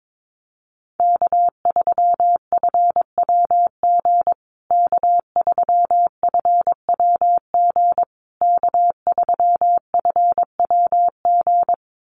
Morse code, heard as a keyed tone, about 22 wpm, 700 Hz. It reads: K3FWG K3FWG K3FWG
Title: USA Call Signs Repeated 3x 22wpm